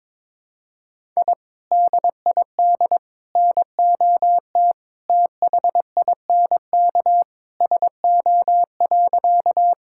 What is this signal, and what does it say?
Morse code, heard as a keyed tone, about 21 wpm, 700 Hz. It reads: I DID NOT THINK SO.
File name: practice / 22wpm / Sentences with 2-Word Phrases - 22wpm